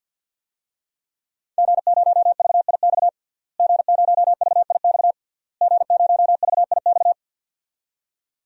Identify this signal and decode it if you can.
Morse code, heard as a keyed tone, about 52 wpm, 700 Hz. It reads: G0VIX G0VIX G0VIX
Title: DXCC-100 Call Signs - 5 Characters - Repeated 3x - 50wpm